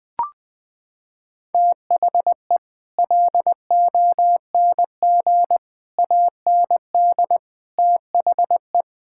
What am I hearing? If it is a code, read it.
Morse code, heard as a keyed tone, about 20 wpm, 700 Hz. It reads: THE LONG AND THE
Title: Sentences with 9-Word Phrases - 20wpm